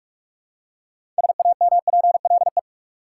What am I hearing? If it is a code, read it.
Morse code, heard as a keyed tone, about 42 wpm, 700 Hz. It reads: SAMPLE
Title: Six Letter Words - 45wpm